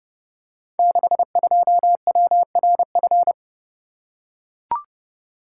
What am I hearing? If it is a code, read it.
Morse code, heard as a keyed tone, about 29 wpm, 700 Hz. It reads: 62WRF
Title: Code Groups - 5 Letter:Numbers - 30wpm